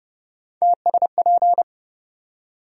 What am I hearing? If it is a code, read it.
Morse code, heard as a keyed tone, about 30 wpm, 700 Hz. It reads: TSP
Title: USA Call Sign Postfixes - 3 Letters 30wpm